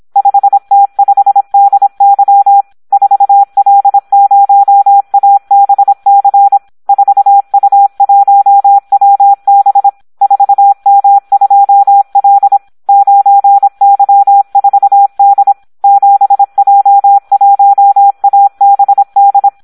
Morse code, heard as a keyed tone, about 26 wpm, 800 Hz. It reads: 5T5DY 4L0ABC 4U1WB 4M2L 9Y4D 7J1ABD
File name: Audio practica CW / SD 1 26